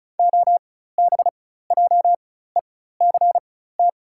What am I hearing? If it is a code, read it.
Morse code, heard as a keyed tone, about 34 wpm, spaced as for 16 wpm, 690 Hz. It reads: OBJECT
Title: 4x Character Spacing - Top 500 Words - 35wpm